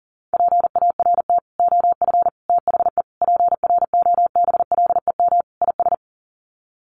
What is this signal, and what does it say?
Morse code, heard as a keyed tone, about 40 wpm, 700 Hz. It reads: PART OF THE PROBLEM IS
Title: Sets of 5 Words Encore2 40wpm